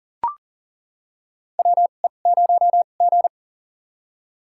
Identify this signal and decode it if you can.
Morse code, heard as a keyed tone, about 40 wpm, spaced as for 29 wpm, 690 Hz. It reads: WE0G
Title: USA Call Signs - 4 Characters - 2x Character Spacing - 40wpm